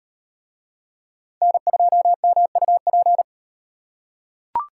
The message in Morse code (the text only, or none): N2MUP